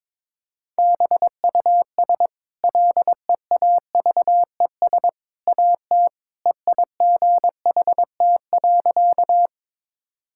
Morse code, heard as a keyed tone, about 22 wpm, 690 Hz. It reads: BUS LEAVES AT EIGHT.
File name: practice / 22wpm / Easy Sentences with Hints - 22wpm